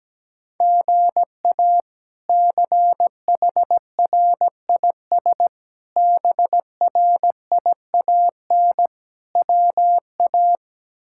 Morse code, heard as a keyed tone, about 17 wpm, 700 Hz. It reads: GA CHRIS BRIAN WA
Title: K1USN SST QSO Practice - 17wpm